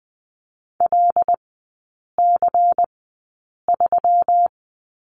Morse code, heard as a keyed tone, about 20 wpm, spaced as for 8 wpm, 700 Hz. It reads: LC3
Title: 184 Lesson 30e - Letter b - Three Characters 20wpm